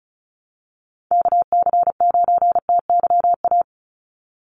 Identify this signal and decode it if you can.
Morse code, heard as a keyed tone, about 34 wpm, 700 Hz. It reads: KC9TYA
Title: NA SOTA Chaser Callsigns 35wpm